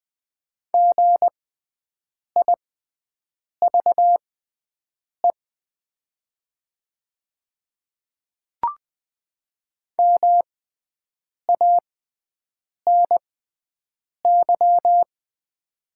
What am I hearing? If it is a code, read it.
Morse code, heard as a keyed tone, about 20 wpm, spaced as for 7 wpm, 700 Hz. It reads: GIVE MANY
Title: 6x Character Spacing - Top 300 Words - 20wpm